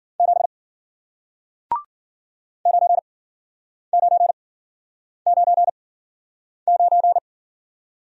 Morse code, heard as a keyed tone, about 54 wpm, spaced as for 9 wpm, 700 Hz. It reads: 7 9999